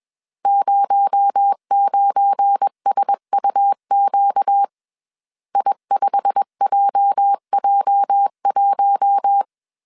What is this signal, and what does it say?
Morse code, heard as a keyed tone, about 21 wpm, 800 Hz. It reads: 09SUQ I5JJ1
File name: CW / DET / 1 9 1500C S C 20wpm 2310